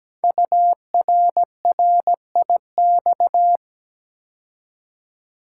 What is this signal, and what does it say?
Morse code, heard as a keyed tone, about 17 wpm, 700 Hz. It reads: URRIX
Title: Code Groups - 5 Letters - 17wpm